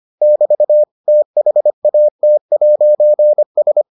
Morse code, heard as a keyed tone, about 25 wpm, 600 Hz. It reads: <BT> THAT'S